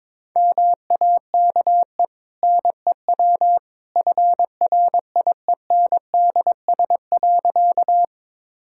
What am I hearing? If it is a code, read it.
Morse code, heard as a keyed tone, about 21 wpm, 690 Hz. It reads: MAKE NEW FRIENDS.
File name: practice / 22wpm / Easy Sentences with Hints - 22wpm